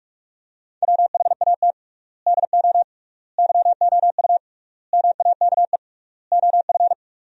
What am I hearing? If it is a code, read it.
Morse code, heard as a keyed tone, about 46 wpm, 700 Hz. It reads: WHAT DO YOU MAKE OF
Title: 3x Word-Spacing - Sentences from Top 100 Words - 45wpm